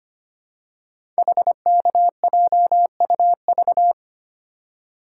Morse code, heard as a keyed tone, about 25 wpm, 700 Hz. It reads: HKJUV
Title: Code Groups - 5 Letters - 25wpm